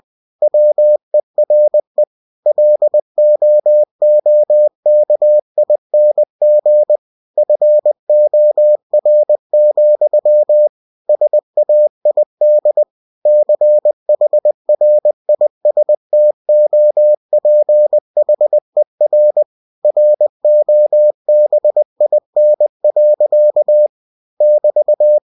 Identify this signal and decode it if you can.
Morse code, heard as a keyed tone, about 20 wpm, 600 Hz. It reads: WERE LOOKING FOR, SAID CHRISTOPHER ROBIN. <BT>
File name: cw-books / winnie-the-pooh / winnie-20wpm-page-0085